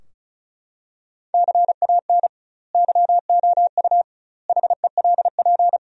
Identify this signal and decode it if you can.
Morse code, heard as a keyed tone, about 36 wpm, 700 Hz. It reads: CAN YOU HELP
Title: 2x Word-Spacing - Sentences from Top 200 Words - 35wpm